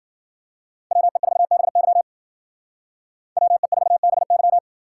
Morse code, heard as a keyed tone, about 50 wpm, 700 Hz. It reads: WE4BY WE4BY
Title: Speed Racing - USA 2x2 Call Signs 40wpm